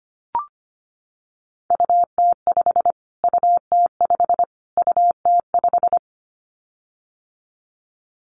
Morse code, heard as a keyed tone, about 25 wpm, 700 Hz. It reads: UT5 UT5 UT5
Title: Callsign Prefixes for Top 25 DXCC - Repeated 3x - 25wpm